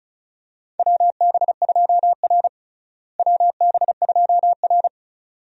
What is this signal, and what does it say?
Morse code, heard as a keyed tone, about 34 wpm, 700 Hz. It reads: WB2R WB2R
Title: USA Call Signs - 4 Characters - Repeated 2x - 35wpm